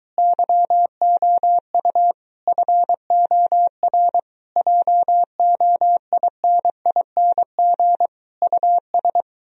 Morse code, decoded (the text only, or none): YOU FOR JOINING US